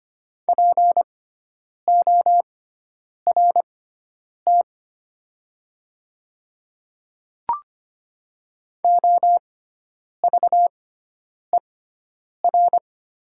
Morse code, heard as a keyed tone, about 25 wpm, spaced as for 9 wpm, 700 Hz. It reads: PORT OVER
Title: 6x Character Spacing - Top 200 Words - 25wpm